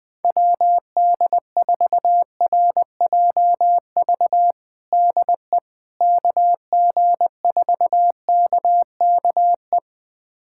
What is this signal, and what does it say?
Morse code, heard as a keyed tone, about 20 wpm, 700 Hz. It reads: WD4RJV DE KG4KKE
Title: USA Callsign DE Callsign - 6 Characters - 20wpm